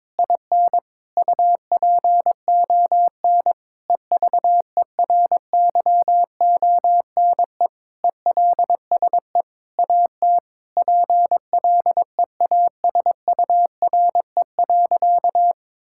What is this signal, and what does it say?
Morse code, heard as a keyed tone, about 22 wpm, 700 Hz. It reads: IN UPON EVERYONE ELSE AT PLEASURE.
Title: Sentences from Top 1600 Words 22wpm